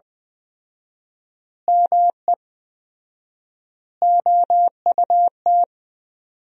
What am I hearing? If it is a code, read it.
Morse code, heard as a keyed tone, about 21 wpm, 700 Hz. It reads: ME OUT
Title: 4x Word-Spacing - Sentences from Top 400 Words - 20wpm